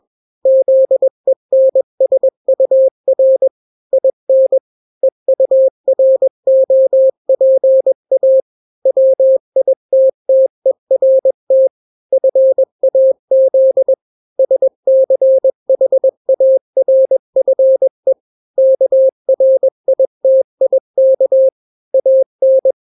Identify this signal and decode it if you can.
Morse code, heard as a keyed tone, about 21 wpm, 530 Hz. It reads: ZENSUR IN EUROPA WITTERT FAZ SCHARFE KRITIK AN
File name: daily / 20wpm